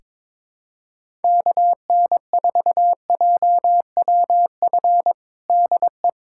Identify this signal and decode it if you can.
Morse code, heard as a keyed tone, about 22 wpm, 700 Hz. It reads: KN4JWF DE
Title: USA Callsign DE Callsign 22wpm